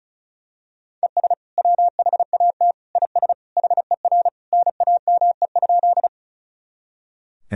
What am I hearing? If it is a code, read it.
Morse code, heard as a keyed tone, about 36 wpm, 700 Hz. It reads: ES WHAT IS HER NAME?
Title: Sentences from Top 200 Words and Abbreviations 35wpm